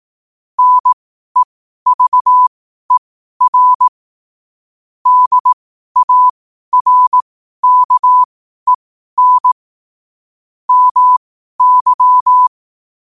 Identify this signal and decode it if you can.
Morse code, heard as a keyed tone, about 18 wpm, spaced as for 12 wpm, 1000 Hz. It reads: NEVER DARKEN MY
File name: qotdfiles / 12wpm